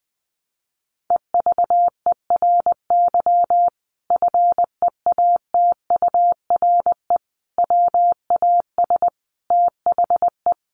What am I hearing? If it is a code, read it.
Morse code, heard as a keyed tone, about 20 wpm, 700 Hz. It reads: EVERY FEATURE WAS THE
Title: Sentences from Top 800 Words 20wpm